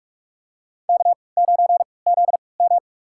Morse code, heard as a keyed tone, about 43 wpm, spaced as for 25 wpm, 700 Hz. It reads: K9ZM
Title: USA Call Signs - 4 Characters - 3x Character Spacing - 45wpm